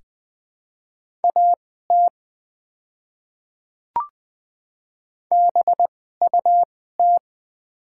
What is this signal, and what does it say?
Morse code, heard as a keyed tone, about 20 wpm, spaced as for 14 wpm, 700 Hz. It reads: AT BUT
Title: 2x Character Spacing - Top 300 Words - 20wpm